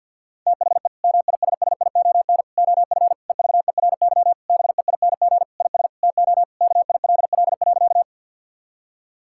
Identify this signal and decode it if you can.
Morse code, heard as a keyed tone, about 50 wpm, 700 Hz. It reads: THE MISSION OF EVERY BEING IS TO KILL.
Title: Sentences from Top 1600 Words 50wpm